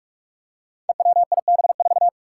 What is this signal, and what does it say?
Morse code, heard as a keyed tone, about 44 wpm, 700 Hz. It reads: EWIB4
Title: Five Challenging Letter-Number Combinations 45wpm